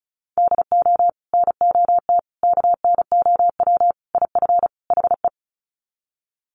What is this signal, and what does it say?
Morse code, heard as a keyed tone, about 34 wpm, 700 Hz. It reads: DO NOT KNOW IF HE
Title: Sets of 5 Words Encore2 35wpm